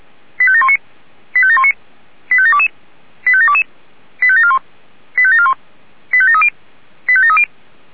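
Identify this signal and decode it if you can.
Five-tone selective call (ZVEI), heard as quick runs of five tones, tones of about 70 ms. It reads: 85719 85719 85722 85722 85741 85741 85730 85730
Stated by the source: ZVEI five-tone selective call received by SDR.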